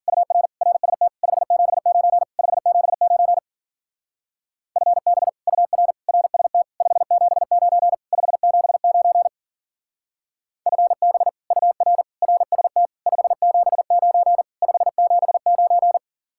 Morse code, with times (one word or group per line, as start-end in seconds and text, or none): 0.1-0.5 UR
0.6-1.1 RST
1.2-2.2 579
2.4-3.4 579
4.8-5.3 FB
5.5-5.9 UR
6.1-6.6 RST
6.8-7.9 579
8.1-9.3 579
10.7-11.3 FB
11.5-12.0 UR
12.2-12.9 RST
13.1-14.4 579
14.6-16.0 579